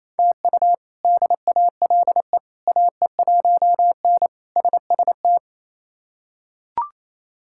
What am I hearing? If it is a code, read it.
Morse code, heard as a keyed tone, about 28 wpm, 700 Hz. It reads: TU DALE AE1N SST
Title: K1USN SST QSO Practice - 28wpm